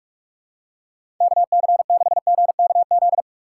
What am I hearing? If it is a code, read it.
Morse code, heard as a keyed tone, about 46 wpm, 700 Hz. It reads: KC6GKZ
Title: USA Call Signs - 4x Character Spacing - 45wpm